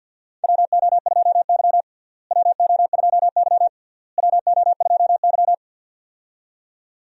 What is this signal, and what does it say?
Morse code, heard as a keyed tone, about 49 wpm, 700 Hz. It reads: WO2Y WO2Y WO2Y
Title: USA Call Signs - 4 Characters - Repeated 3x - 50wpm